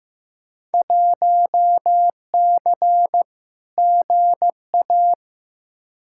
Morse code, heard as a keyed tone, about 15 wpm, 700 Hz. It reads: 1C GA
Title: Field Day Exchanges - 15wpm